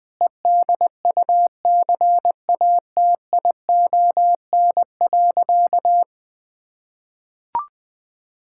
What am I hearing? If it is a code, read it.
Morse code, heard as a keyed tone, about 20 wpm, 700 Hz. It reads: EDUCATION.
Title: Sentences from Top 1000 Words - Review Part 6 - 20wpm